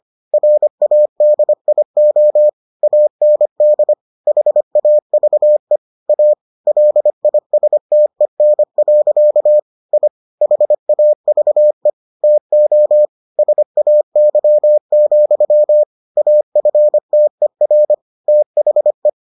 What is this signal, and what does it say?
Morse code, heard as a keyed tone, about 25 wpm, 600 Hz. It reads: RADIO AND HAVE A LISTEN. I HAVE TO SAY, AFTER THE